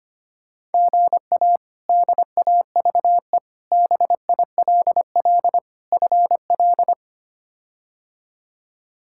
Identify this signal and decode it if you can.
Morse code, heard as a keyed tone, about 25 wpm, 700 Hz. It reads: GA DAVE BILL FL
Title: K1USN SST QSO Practice - 25wpm